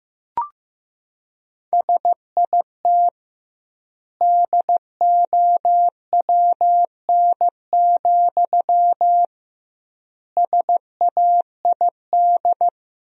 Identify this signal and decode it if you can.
Morse code, heard as a keyed tone, about 15 wpm, 700 Hz. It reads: SIT DOWN, SAID
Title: 2x Word-Spacing - Sentences from Top 500 Words - 15wpm